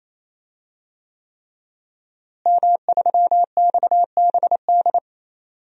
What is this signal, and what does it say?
Morse code, heard as a keyed tone, about 28 wpm, 700 Hz. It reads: M3XBD
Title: DXCC-100 Call Signs - 5 Characters - Repeated 3x - 28wpm